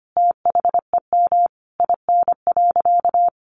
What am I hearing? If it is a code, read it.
Morse code, heard as a keyed tone, about 25 wpm, 700 Hz. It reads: THEM IN.
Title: Sentences from Top 500 Words 25wpm